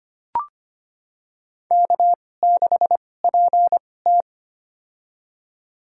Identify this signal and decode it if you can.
Morse code, heard as a keyed tone, about 25 wpm, spaced as for 18 wpm, 700 Hz. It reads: K6PT
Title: USA Call Signs - 4 Characters - 2x Character Spacing - 25wpm